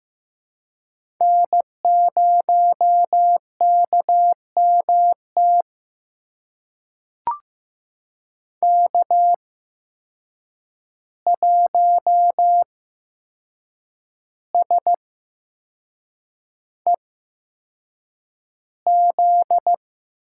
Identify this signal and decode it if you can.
Morse code, heard as a keyed tone, about 15 wpm, spaced as for 6 wpm, 700 Hz. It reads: N0KMT K1SEZ
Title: USA Call Signs - 5 Characters - 8x Character Spacing - 15wpm